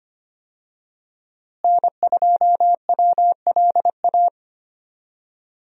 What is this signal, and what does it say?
Morse code, heard as a keyed tone, about 25 wpm, 700 Hz. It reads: N2WLA
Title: USA Call Signs - 5 Characters - Repeated 2x - 25wpm